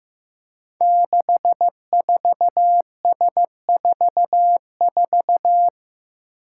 Morse code, heard as a keyed tone, about 15 wpm, 700 Hz. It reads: 64S44
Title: Five Challenging Letter-Number Combinations 15wpm